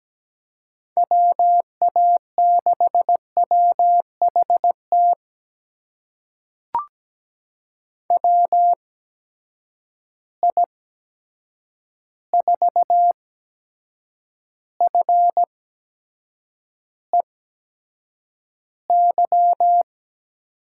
Morse code, heard as a keyed tone, about 17 wpm, spaced as for 7 wpm, 700 Hz. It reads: WA6WHT WI4FEY
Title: USA Call Signs - 6 Characters - 8x Character Spacing - 17wpm